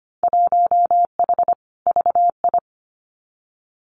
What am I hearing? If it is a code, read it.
Morse code, heard as a keyed tone, about 25 wpm, 700 Hz. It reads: 1H VI
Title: Winter Field Day Exchanges Encore 25wpm